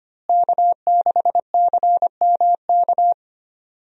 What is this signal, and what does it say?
Morse code, heard as a keyed tone, about 25 wpm, 700 Hz. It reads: K6CMK
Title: USA Call Signs - 5 Characters - Repeated 3x - 25wpm